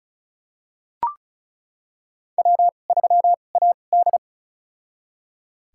Morse code, heard as a keyed tone, about 36 wpm, spaced as for 25 wpm, 700 Hz. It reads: W3AD
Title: USA Call Signs - 4 Characters - 2x Character Spacing - 35wpm